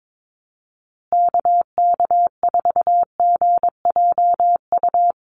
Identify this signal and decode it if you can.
Morse code, heard as a keyed tone, about 22 wpm, 700 Hz. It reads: KK4GJU
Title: USA Call Signs Repeated 3x 22wpm